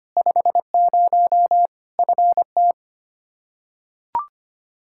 Morse code, heard as a keyed tone, about 25 wpm, 700 Hz. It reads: 50 FT
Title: Practice QSO - Exchange 2 - Info - 25wpm